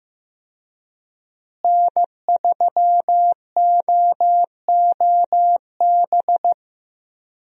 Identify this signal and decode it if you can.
Morse code, heard as a keyed tone, about 15 wpm, 700 Hz. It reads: N3OOB
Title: USA Call Signs - 5 Characters - Repeated 3x - 15wpm